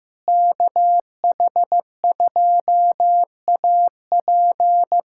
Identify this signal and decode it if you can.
Morse code, heard as a keyed tone, about 16 wpm, 700 Hz. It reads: KH2AP
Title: USA Callsign DE Callsign - 5 Characters - 15wpm